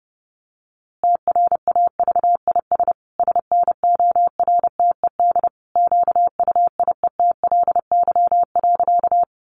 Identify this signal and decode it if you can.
Morse code, heard as a keyed tone, about 28 wpm, 700 Hz. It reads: TRAVIS SNORTED QUIETLY.
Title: A Sound Of Thunder 30wpm